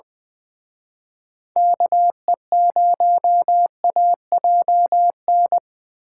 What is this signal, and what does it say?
Morse code, heard as a keyed tone, about 25 wpm, 700 Hz. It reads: KE0AJN
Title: Speed Racing - USA 2x3 Call Signs 20wpm